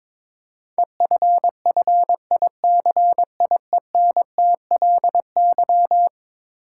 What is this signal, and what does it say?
Morse code, heard as a keyed tone, about 22 wpm, 700 Hz. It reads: EFFICIENTLY
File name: practice / 22wpm / Eleven Letter Words - 22wpm